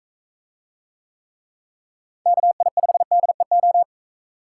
Morse code, heard as a keyed tone, about 44 wpm, 700 Hz. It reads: KI5DEO